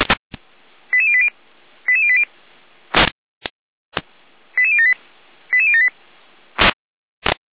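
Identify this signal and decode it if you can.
Five-tone selective call (ZVEI), heard as quick runs of five tones, tones of about 70 ms. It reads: 88080 88080 88078 88078
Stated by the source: ZVEI five-tone selective call received on SDR.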